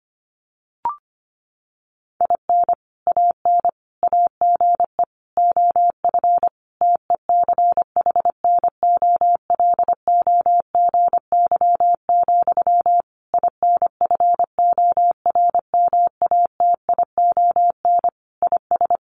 Morse code, heard as a keyed tone, about 25 wpm, 700 Hz. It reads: IN AN AGE OF TECHNOLOGY, INFORMATION IS
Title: Sentences from Top 1200 Words 25wpm